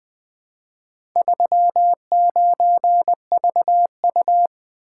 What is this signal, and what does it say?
Morse code, heard as a keyed tone, about 20 wpm, 700 Hz. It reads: 39VU
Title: Four Letter-Number Combinations - 20wpm